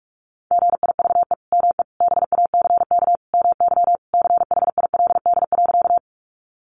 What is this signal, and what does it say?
Morse code, heard as a keyed tone, about 44 wpm, 700 Hz. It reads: GIVE ME BACK MY CHILD.
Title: Sentences from Top 800 Words 45wpm